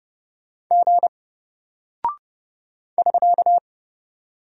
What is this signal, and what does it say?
Morse code, heard as a keyed tone, about 30 wpm, 700 Hz. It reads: G <SK>
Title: Single Character (Letter:Number:Punctuation:Prosign) Alt 30wpm